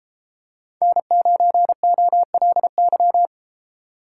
Morse code, heard as a keyed tone, about 32 wpm, 700 Hz. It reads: N9OLY